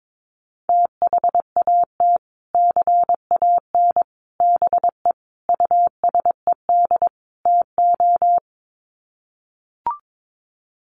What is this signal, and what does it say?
Morse code, heard as a keyed tone, about 22 wpm, 700 Hz. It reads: THAT CAN BE USED TO
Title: Sets of 5 Words Encore 22wpm